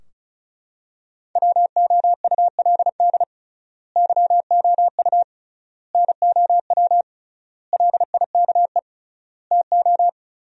Morse code, heard as a keyed tone, about 34 wpm, 700 Hz. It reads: WOULD YOU NOW LIKE TO
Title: 3x Word-Spacing - Sentences from Top 200 Words - 35wpm